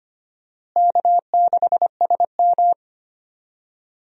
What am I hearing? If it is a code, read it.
Morse code, heard as a keyed tone, about 25 wpm, 700 Hz. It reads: K6SM
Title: USA Call Signs - 4 Characters - Repeated 2x - 25wpm